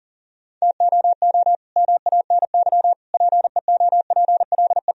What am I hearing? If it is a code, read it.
Morse code, heard as a keyed tone, about 40 wpm, 700 Hz. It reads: TOO MANY PEOPLE
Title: Sentences from Top 1000 Words - Review Part 1 - 40wpm